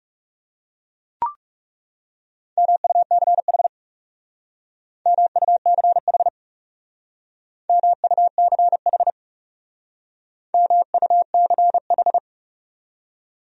Morse code, heard as a keyed tone, about 36 wpm, 700 Hz. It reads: MUCH MUCH MUCH MUCH
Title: Speed-Racing Top 300 Words 30wpm